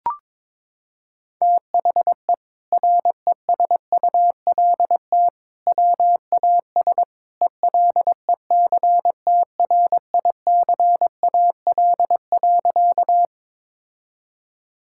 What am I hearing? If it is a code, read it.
Morse code, heard as a keyed tone, about 22 wpm, 700 Hz. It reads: THE RESULT WAS ELECTRICAL.
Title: Sentences from Top 1000 Words - Review Part 6 - 22wpm